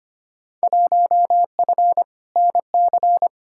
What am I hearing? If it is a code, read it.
Morse code, heard as a keyed tone, about 25 wpm, 700 Hz. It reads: 1F NC
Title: Field Day Exchanges - 25wpm